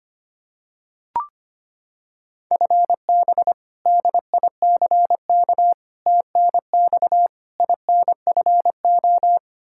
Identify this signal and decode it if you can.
Morse code, heard as a keyed tone, about 25 wpm, 700 Hz. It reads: FB DICK TNX INFO
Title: Practice QSO - Exchange 3 - More Info - 25wpm